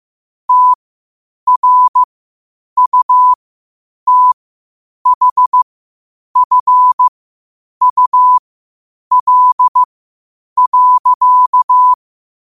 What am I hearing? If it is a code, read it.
Morse code, heard as a keyed tone, about 15 wpm, spaced as for 8 wpm, 1000 Hz. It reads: TRUTHFUL.
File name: qotdfiles / 8wpm